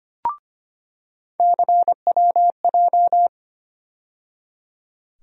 Morse code, heard as a keyed tone, about 25 wpm, 700 Hz. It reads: CWJ
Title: USA Call Sign Postfixes - 3 Letters 25wpm